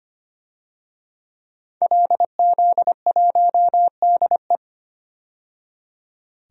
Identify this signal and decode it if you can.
Morse code, heard as a keyed tone, about 25 wpm, 700 Hz. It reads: LZ1DE
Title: DXCC-100 Call Signs - 5 Characters - 25wpm